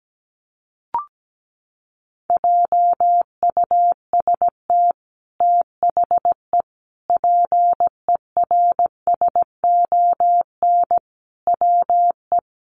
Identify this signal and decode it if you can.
Morse code, heard as a keyed tone, about 17 wpm, 700 Hz. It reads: JUST THE PERSON WE
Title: Sentences from Top 900 Words 17wpm